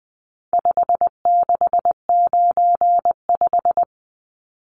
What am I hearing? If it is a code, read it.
Morse code, heard as a keyed tone, about 20 wpm, 700 Hz. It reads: 5695
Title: Numbers - 4-Digit 20wpm